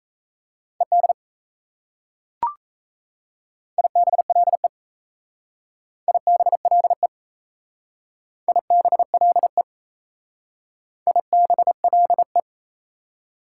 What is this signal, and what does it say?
Morse code, heard as a keyed tone, about 34 wpm, 700 Hz. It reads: ED IBLE IBLE IBLE IBLE
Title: Speed-Racing Prefixes and Suffixes 28wpm